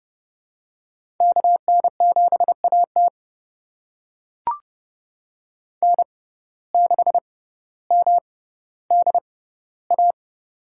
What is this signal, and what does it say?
Morse code, heard as a keyed tone, about 30 wpm, spaced as for 15 wpm, 700 Hz. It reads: KN7AT N6MDA
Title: USA Call Signs - 5 Characters - 6x Character Spacing - 30wpm